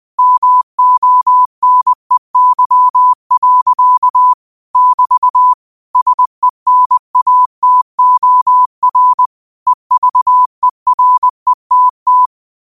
Morse code, heard as a keyed tone, about 20 wpm, 1000 Hz. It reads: MONEY. <BT> SENATOR EVERETT
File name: qotdfiles / 20wpm